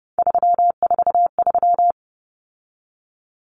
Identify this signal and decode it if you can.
Morse code, heard as a keyed tone, about 30 wpm, 700 Hz. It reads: 343
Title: Common Cut Numbers - 3-Digit 30wpm